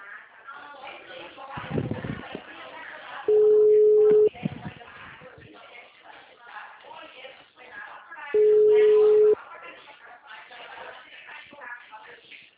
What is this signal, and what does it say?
Ringback tone (Europe, 425 Hz), with speech.
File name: out- -2017- - - .76384